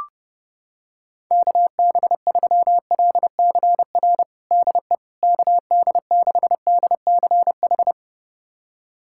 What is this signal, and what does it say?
Morse code, heard as a keyed tone, about 30 wpm, 700 Hz. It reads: KB3LCR DE KD6DCH
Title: USA Callsign DE Callsign - 6 Characters - 30wpm